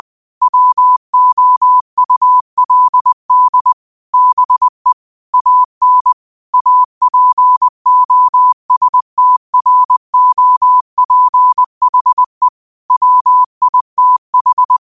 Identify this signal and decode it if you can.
Morse code, heard as a keyed tone, about 20 wpm, 1000 Hz. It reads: WOULD BE AN APOSTROPHE WITH
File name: qotdfiles / 20wpm